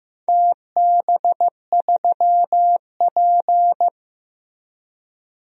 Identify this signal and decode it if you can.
Morse code, heard as a keyed tone, about 15 wpm, 700 Hz. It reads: TB3P
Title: DXCC-100 Call Signs - 4 Characters - Repeated 3x - 15wpm